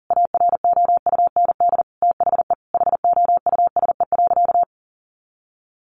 Morse code, heard as a keyed tone, about 40 wpm, 700 Hz. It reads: AROUND THE HOUSE.
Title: Easy Phrases 40wpm